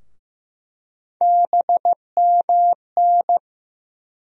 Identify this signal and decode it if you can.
Morse code, heard as a keyed tone, about 15 wpm, 700 Hz. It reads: BMN
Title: USA Call Sign Postfixes - 3 Letters 15wpm